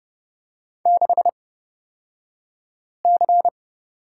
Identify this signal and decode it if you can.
Morse code, heard as a keyed tone, about 30 wpm, spaced as for 5 wpm, 700 Hz. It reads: 6C
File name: Lesson 37d - Letter x - Two Characters 30wpm